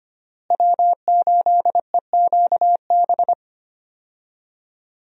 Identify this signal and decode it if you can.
Morse code, heard as a keyed tone, about 25 wpm, 700 Hz. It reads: W8EQB
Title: USA Call Signs - 5 Characters - Repeated 2x - 25wpm